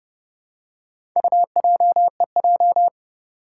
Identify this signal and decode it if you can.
Morse code, heard as a keyed tone, about 30 wpm, 700 Hz. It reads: UJEJ
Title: Four Challenging Letter-Number Combinations 30wpm